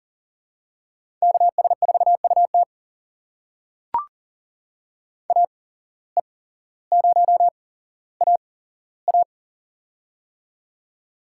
Morse code, heard as a keyed tone, about 39 wpm, spaced as for 17 wpm, 700 Hz. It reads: KS4UT AE0AA
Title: USA Call Signs - 5 Characters - 8x Character Spacing - 40wpm